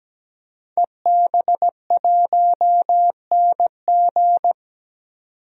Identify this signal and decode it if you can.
Morse code, heard as a keyed tone, about 17 wpm, 700 Hz. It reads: EB1NG
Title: DXCC-100 Call Signs - 5 Characters - 17wpm